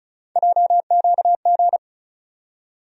Morse code, heard as a keyed tone, about 34 wpm, 700 Hz. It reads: JQG
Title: USA Call Sign Postfixes - 3 Letters 35wpm